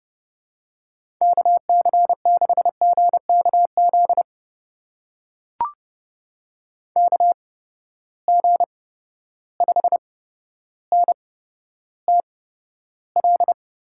Morse code, heard as a keyed tone, about 30 wpm, spaced as for 13 wpm, 700 Hz. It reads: KC6GKZ KG5NTL
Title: USA Call Signs - 6 Characters - 8x Character Spacing - 30wpm